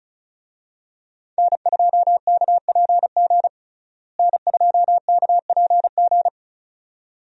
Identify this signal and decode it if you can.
Morse code, heard as a keyed tone, about 36 wpm, 700 Hz. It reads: N2KPG N2KPG
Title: USA Call Signs - 5 Characters - Repeated 2x - 35wpm